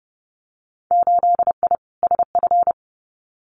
Morse code, heard as a keyed tone, about 30 wpm, 700 Hz. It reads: 8I SF
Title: Winter Field Day Exchanges Encore 30wpm